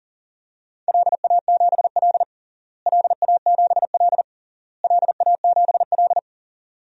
Morse code, heard as a keyed tone, about 40 wpm, 700 Hz. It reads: LA7L LA7L LA7L
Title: DXCC-100 Call Signs - 4 Characters - Repeated 3x - 40wpm